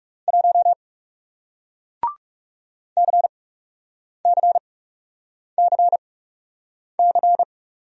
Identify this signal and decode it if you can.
Morse code, heard as a keyed tone, about 39 wpm, spaced as for 8 wpm, 700 Hz. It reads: 1 CCCC